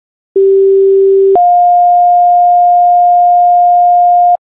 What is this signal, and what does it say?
Two-tone page, heard as a one-second tone, then a three-second tone, about 390 then 710 Hz. It is heard from 0.4 to 4.4 s.